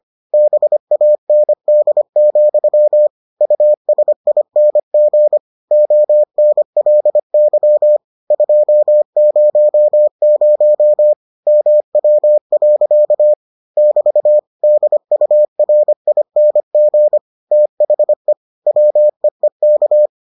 Morse code, heard as a keyed tone, about 25 wpm, 600 Hz. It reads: BAND, USING ONLY 200 MW. <BT> DURING THE WEEK